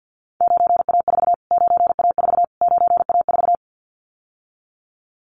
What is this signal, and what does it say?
Morse code, heard as a keyed tone, about 50 wpm, 700 Hz. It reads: 9A4 9A4 9A4
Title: Callsign Prefixes for Top 25 DXCC - Repeated 3x - 50wpm